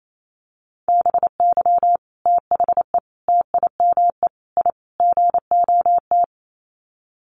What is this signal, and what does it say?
Morse code, heard as a keyed tone, about 28 wpm, 700 Hz. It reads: BY THE TIME I GOT
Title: Sets of 5 Words Encore2 28wpm